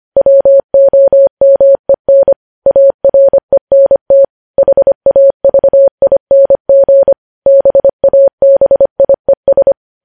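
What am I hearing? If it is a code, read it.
Morse code, heard as a keyed tone, about 25 wpm, 550 Hz. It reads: WOMEN ARENT HAVING BABIES
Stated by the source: Morse code news headlines sent at 25 wpm.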